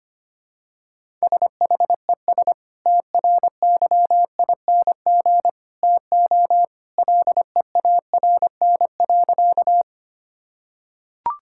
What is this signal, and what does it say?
Morse code, heard as a keyed tone, about 25 wpm, 700 Hz. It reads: SHES TRYING TO LEARN.
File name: practice / 25wpm / Sentences with 2-Word Phrases - 25wpm